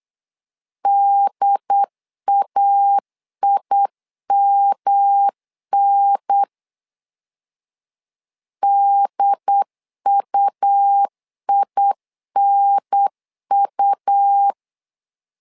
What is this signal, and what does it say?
Morse code, heard as a keyed tone, about 8 wpm, 800 Hz. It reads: DAIMN DUINU
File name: LEZIONI / 3 MWRO / 3 M 8wpm 2300